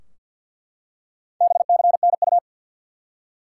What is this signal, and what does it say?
Morse code, heard as a keyed tone, about 50 wpm, 700 Hz. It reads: BCNU
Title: Top 180 QSO Elements 50wpm